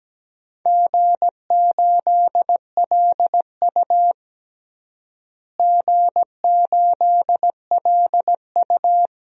Morse code, heard as a keyed tone, about 17 wpm, 700 Hz. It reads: G8LU G8LU
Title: DXCC-100 Call Signs - 4 Characters - Repeated 2x - 17wpm